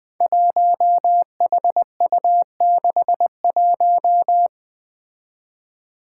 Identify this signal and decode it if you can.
Morse code, heard as a keyed tone, about 20 wpm, 700 Hz. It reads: 1HU61
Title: Five Challenging Letter-Number Combinations 20wpm